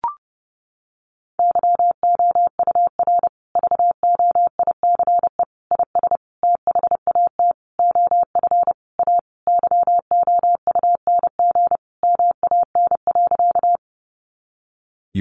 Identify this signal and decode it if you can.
Morse code, heard as a keyed tone, about 28 wpm, 700 Hz. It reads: YOUR VOICE IS THAT OF A YOUNG MAN.
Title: Sentences from Top 500 Words 30wpm